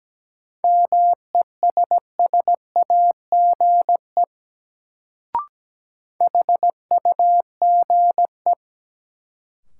Morse code, heard as a keyed tone, about 17 wpm, 700 Hz. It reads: MESSAGE HUGE
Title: Sentences from Top 1100 Words - New Word Drill Alt - 17wpm